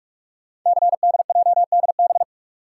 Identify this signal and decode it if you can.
Morse code, heard as a keyed tone, about 44 wpm, 700 Hz. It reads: CDJDB
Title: Code Groups - 5 Letters - 45wpm